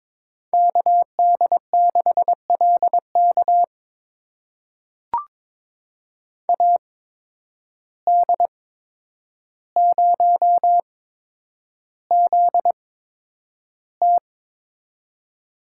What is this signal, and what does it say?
Morse code, heard as a keyed tone, about 22 wpm, spaced as for 9 wpm, 700 Hz. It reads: KD6LK AD0ZT
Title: USA Call Signs - 8x Character Spacing - 22wpm